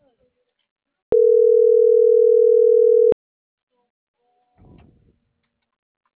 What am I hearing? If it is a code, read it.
Ringback tone (North America), 440 + 480 Hz.